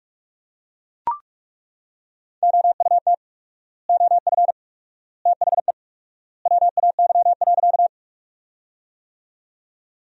Morse code, heard as a keyed tone, about 42 wpm, 700 Hz. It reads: OUT OF THE WAY.
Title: 4x Word-Spacing - Sentences from Top 100 Words - 45wpm